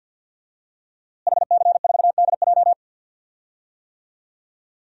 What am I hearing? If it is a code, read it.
Morse code, heard as a keyed tone, about 51 wpm, 700 Hz. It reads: HC4DJ
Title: DXCC-100 Call Signs - 5 Characters - Repeated 3x - 50wpm